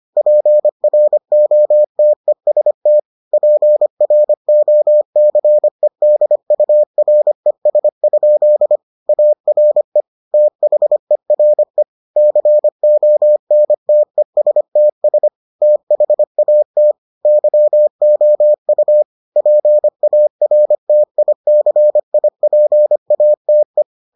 Morse code, heard as a keyed tone, about 25 wpm, 600 Hz. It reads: PROTEST PROCEDURES? ARE THERE CONTESTS THAT YOU PARTICIPATE